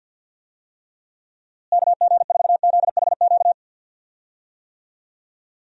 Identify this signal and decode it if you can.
Morse code, heard as a keyed tone, about 46 wpm, 700 Hz. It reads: KG4ZHQ
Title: USA Call Signs - 6 Characters - Repeated 3x - 50wpm